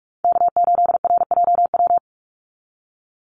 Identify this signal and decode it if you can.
Morse code, heard as a keyed tone, about 44 wpm, 700 Hz. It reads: K8RJW
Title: Call Signs - 45wpm